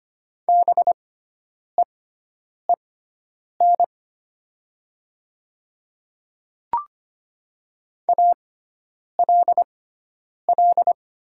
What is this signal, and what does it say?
Morse code, heard as a keyed tone, about 25 wpm, spaced as for 9 wpm, 700 Hz. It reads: BEEN ALL
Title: 6x Character Spacing - Top 100 Words - 25wpm